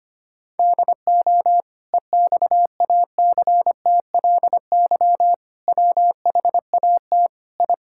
Morse code, heard as a keyed tone, about 25 wpm, 700 Hz. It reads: DO EXACTLY WHAT I
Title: Sentences from Top 1000 Words - Review Part 6 - 25wpm